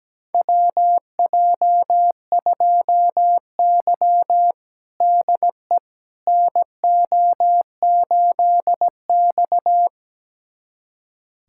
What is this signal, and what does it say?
Morse code, heard as a keyed tone, about 17 wpm, 700 Hz. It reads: WJ2Y DE NO8X
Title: USA Callsign DE Callsign - 4 Characters - 17wpm